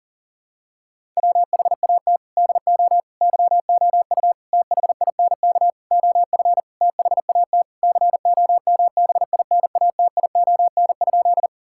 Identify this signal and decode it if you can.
Morse code, heard as a keyed tone, about 40 wpm, 700 Hz. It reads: WHAT DO YOU THINK OF THAT COMBINATION?
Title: Sentences from Top 1000 Words - Review Part 6 - 40wpm